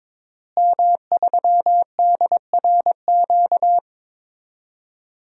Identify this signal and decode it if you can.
Morse code, heard as a keyed tone, about 22 wpm, 700 Hz. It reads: M3DRQ
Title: DXCC-100 Call Signs - 5 Characters - Repeated 2x - 22wpm